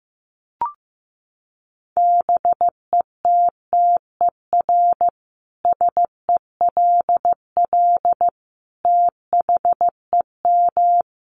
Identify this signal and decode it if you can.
Morse code, heard as a keyed tone, about 15 wpm, 700 Hz. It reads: BETTER SELL THEM
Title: Sentences from Top 1200 Words 15wpm